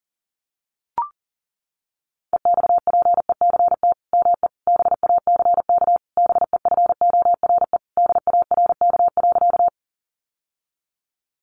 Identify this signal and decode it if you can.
Morse code, heard as a keyed tone, about 40 wpm, 700 Hz. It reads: EXPECT ME BACK BEFORE DARK.
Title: Sentences from Top 1100 Words 40wpm